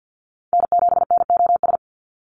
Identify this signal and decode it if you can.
Morse code, heard as a keyed tone, about 49 wpm, 700 Hz. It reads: N7NOS
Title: USA Call Signs 50wpm